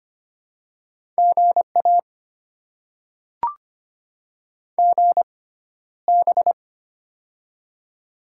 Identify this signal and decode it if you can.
Morse code, heard as a keyed tone, about 25 wpm, spaced as for 13 wpm, 700 Hz. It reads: GA GB
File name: Top 90 QSO Elements - 6x Character Spacing - 25wpm